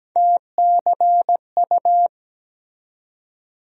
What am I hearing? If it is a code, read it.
Morse code, heard as a keyed tone, about 17 wpm, 700 Hz. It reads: TCU
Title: USA Call Sign Postfixes - 3 Letters 17wpm